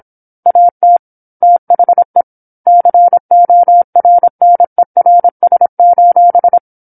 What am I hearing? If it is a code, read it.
Morse code, heard as a keyed tone, about 27 wpm, 700 Hz. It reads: AT THE CORNERS:
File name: 08-Chapter-08 26 WPM